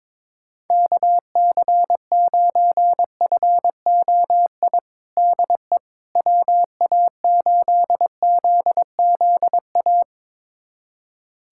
Morse code, heard as a keyed tone, about 22 wpm, 700 Hz. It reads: KC9FOI DE WA8ZZA
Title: USA Callsign DE Callsign - 6 Characters - 22wpm